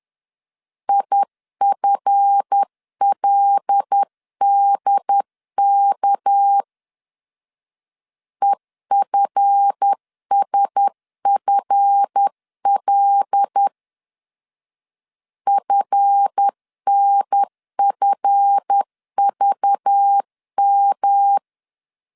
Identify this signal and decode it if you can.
Morse code, heard as a keyed tone, about 10 wpm, 800 Hz. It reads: IFLDK EFSFL FNFVM
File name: LEZIONI / 7 GF9Z / 7 GF 10wpm 2514